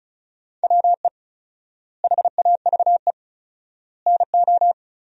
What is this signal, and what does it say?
Morse code, heard as a keyed tone, about 36 wpm, 700 Hz. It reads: WE HAVE NO
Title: 4x Word-Spacing - Sentences from Top 100 Words - 35wpm